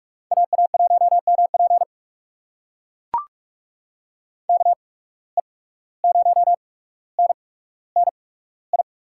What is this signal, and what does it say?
Morse code, heard as a keyed tone, about 44 wpm, spaced as for 18 wpm, 700 Hz. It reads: AA1MP KE0NNI
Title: USA Call Signs - 8x Character Spacing - 45wpm